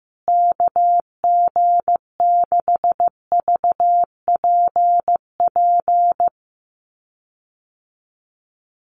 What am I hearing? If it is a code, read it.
Morse code, heard as a keyed tone, about 15 wpm, 700 Hz. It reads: KG6VPP
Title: USA Call Signs Repeated 3x 15wpm